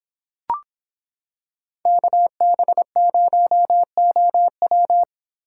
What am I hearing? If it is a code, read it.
Morse code, heard as a keyed tone, about 25 wpm, 700 Hz. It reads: KB0OW